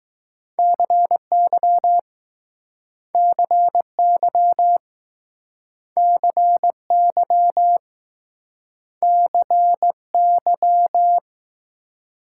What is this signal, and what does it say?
Morse code, heard as a keyed tone, about 20 wpm, 700 Hz. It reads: CY CY CY CY